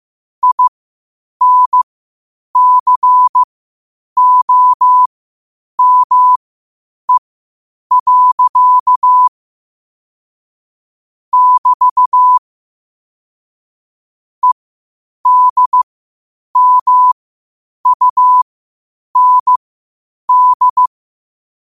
Morse code, heard as a keyed tone, about 15 wpm, spaced as for 8 wpm, 1000 Hz. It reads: INCOME. <BT> EDMUND
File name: qotdfiles / 8wpm